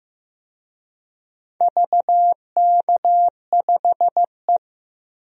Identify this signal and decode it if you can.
Morse code, heard as a keyed tone, about 15 wpm, 700 Hz. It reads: VK5E
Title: DXCC-100 Call Signs - 4 Characters - 15wpm